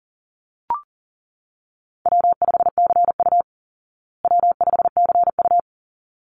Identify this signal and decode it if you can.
Morse code, heard as a keyed tone, about 40 wpm, 700 Hz. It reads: W5CU W5CU
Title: Call Signs Repeated 3x - 40wpm